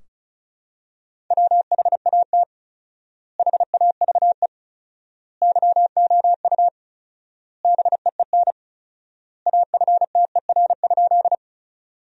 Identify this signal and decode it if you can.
Morse code, heard as a keyed tone, about 36 wpm, 700 Hz. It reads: WHAT HAVE YOU BEEN AFTER?
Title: 4x Word-Spacing - Sentences from Top 200 Words - 35wpm